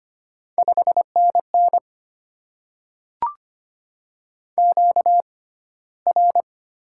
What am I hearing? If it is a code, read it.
Morse code, heard as a keyed tone, about 25 wpm, spaced as for 15 wpm, 700 Hz. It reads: 5NN QR
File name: Top 20 QSO Elements - 6x Character Spacing - 25wpm